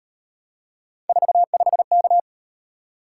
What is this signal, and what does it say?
Morse code, heard as a keyed tone, about 36 wpm, 700 Hz. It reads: 45K